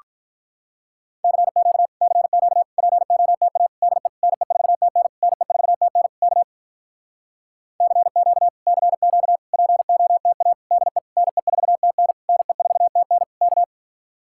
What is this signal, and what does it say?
Morse code, heard as a keyed tone, about 48 wpm, 700 Hz. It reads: CQ CQ POTA DE NE4TN NE4TN K CQ CQ POTA DE NE4TN NE4TN K